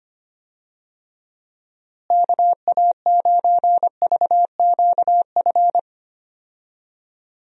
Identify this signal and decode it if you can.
Morse code, heard as a keyed tone, about 25 wpm, 700 Hz. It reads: KA9VQF
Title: USA Call Signs - 6 Characters - Repeated 2x - 25wpm